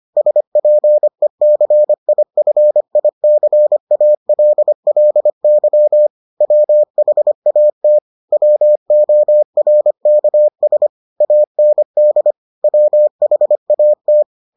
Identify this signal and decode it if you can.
Morse code, heard as a keyed tone, about 25 wpm, 600 Hz. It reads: SPECIFICALLY WHAT WORKS AND WHAT